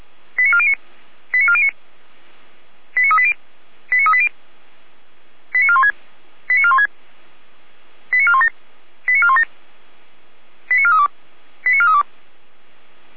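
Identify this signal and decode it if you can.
Five-tone selective call (ZVEI), heard as quick runs of five tones, tones of about 70 ms. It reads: 89409 89409 89380 89380 89416 89416 89417 89417 89432 89432
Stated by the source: Five-tone ZVEI selective call received on an SDR.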